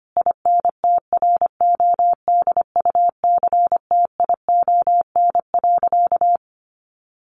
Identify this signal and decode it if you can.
Morse code, heard as a keyed tone, about 25 wpm, 700 Hz. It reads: INTRODUCTION.
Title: Sentences from Top 1000 Words 25wpm